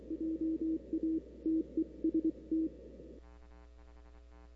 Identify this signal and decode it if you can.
Morse code, heard as a keyed tone, about 23 wpm, 310 Hz. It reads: JA TEST
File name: CW / 2005-k5tr arrl cw dx -03:00